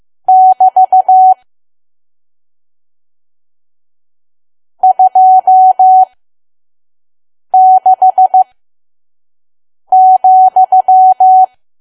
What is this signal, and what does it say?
Morse code, heard as a keyed tone, about 15 wpm, spaced as for 5 wpm, 750 Hz. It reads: <BT> 26,
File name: Archive / 5 WPM files / PM